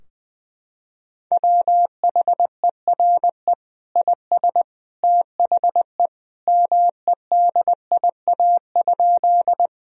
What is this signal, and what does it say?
Morse code, heard as a keyed tone, about 20 wpm, 700 Hz. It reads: WHERE IS THE MEDIA?
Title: Sentences from Top 1000 Words - Review Part 4 - 20wpm